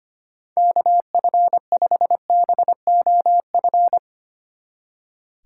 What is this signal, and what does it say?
Morse code, heard as a keyed tone, about 32 wpm, 700 Hz. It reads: KF5BOF